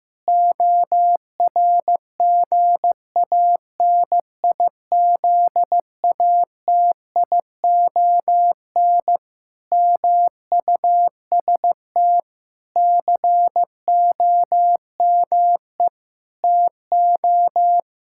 Morse code, heard as a keyed tone, about 15 wpm, 700 Hz. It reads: ORGANIZATION MUST COME TO
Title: Sentences from Top 1000 Words - Review Part 5 - 15wpm